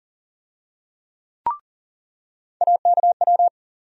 Morse code, heard as a keyed tone, about 40 wpm, 700 Hz. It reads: AKW
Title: USA Call Sign Postfixes - 3 Letters 40wpm